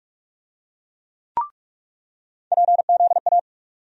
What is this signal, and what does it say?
Morse code, heard as a keyed tone, about 45 wpm, 700 Hz. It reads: PZA
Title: USA Call Sign Postfixes - 3 Letters 45wpm